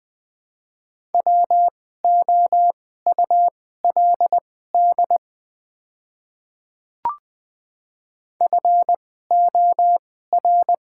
Morse code, heard as a keyed tone, about 20 wpm, spaced as for 14 wpm, 700 Hz. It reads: WOULD FOR
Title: 2x Character Spacing - Top 500 Words - 20wpm